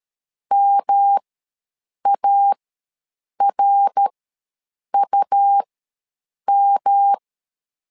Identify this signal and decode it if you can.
Morse code, heard as a keyed tone, about 13 wpm, spaced as for 7 wpm, 800 Hz. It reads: MARUM
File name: LEZIONI / 3 MWRO / 3 MWR ws 12wpm 1809